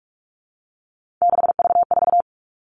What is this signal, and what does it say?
Morse code, heard as a keyed tone, about 44 wpm, 700 Hz. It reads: 6V4
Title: All Cut Numbers - 3-Digit 45wpm